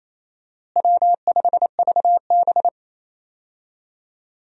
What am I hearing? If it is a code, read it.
Morse code, heard as a keyed tone, about 28 wpm, 700 Hz. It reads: W5VB
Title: USA Call Signs - 4 Characters - Repeated 2x - 28wpm